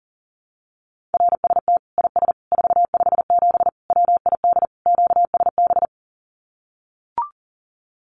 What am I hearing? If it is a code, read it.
Morse code, heard as a keyed tone, about 40 wpm, 700 Hz. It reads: RST IS 457 WID QSB
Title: Common QSO Phrases 40wpm